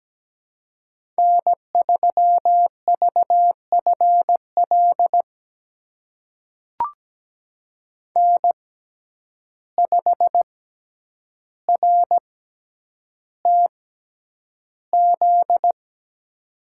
Morse code, heard as a keyed tone, about 17 wpm, spaced as for 9 wpm, 700 Hz. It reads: N3VFL N5RTZ
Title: USA Call Signs - 5 Characters - 6x Character Spacing - 17wpm